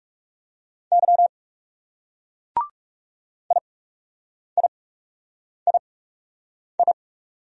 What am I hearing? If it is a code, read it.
Morse code, heard as a keyed tone, about 40 wpm, spaced as for 8 wpm, 700 Hz. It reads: Y IIII